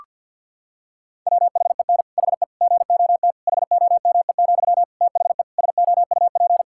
Morse code, heard as a keyed tone, about 52 wpm, 700 Hz. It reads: WHEN HE GOT HOME, THE SOUP
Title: Sentences from Top 1000 Words - Review Part 6 - 50wpm